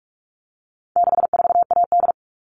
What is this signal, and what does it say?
Morse code, heard as a keyed tone, about 44 wpm, 700 Hz. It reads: 64AD
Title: All Cut Numbers - 4-Digit 45wpm